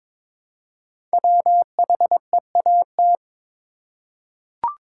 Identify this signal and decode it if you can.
Morse code, heard as a keyed tone, about 22 wpm, 700 Hz. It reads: WHEAT
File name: 5 Letter Words - First 3 Letters Given - 22wpm